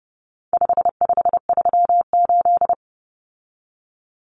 Morse code, heard as a keyed tone, about 30 wpm, 700 Hz. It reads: 5538
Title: Common Cut Numbers - 4-Digit 30wpm